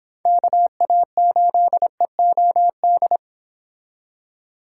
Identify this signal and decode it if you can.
Morse code, heard as a keyed tone, about 26 wpm, 700 Hz. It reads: KA8EOD
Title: Speed Racing - USA Call Signs 22wpm